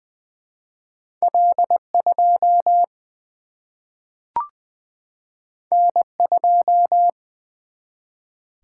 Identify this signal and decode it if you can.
Morse code, heard as a keyed tone, about 20 wpm, 700 Hz. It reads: L2 N2
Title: Single-Letter Single-Number 20wpm